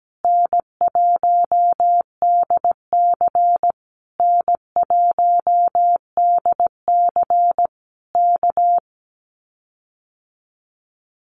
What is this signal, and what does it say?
Morse code, heard as a keyed tone, about 17 wpm, 700 Hz. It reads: N1DC N1DC K
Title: Common QSO Phrases Encore 17wpm